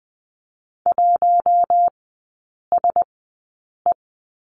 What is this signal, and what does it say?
Morse code, heard as a keyed tone, about 20 wpm, spaced as for 8 wpm, 700 Hz. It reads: 1SE
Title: 205 Lesson 33e - Letter j - Three Characters 20wpm